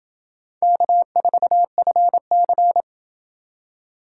Morse code, heard as a keyed tone, about 28 wpm, 700 Hz. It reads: K4FC